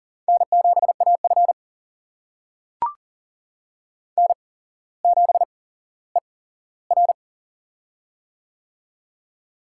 Morse code, heard as a keyed tone, about 37 wpm, spaced as for 17 wpm, 700 Hz. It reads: N7AF N7ER
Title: USA Call Signs - 4 Characters - 8x Character Spacing - 40wpm